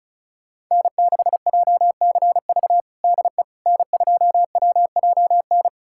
Morse code, heard as a keyed tone, about 36 wpm, 700 Hz. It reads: N6JCV DE N2WJN
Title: USA Callsign DE Callsign - 5 Characters - 35wpm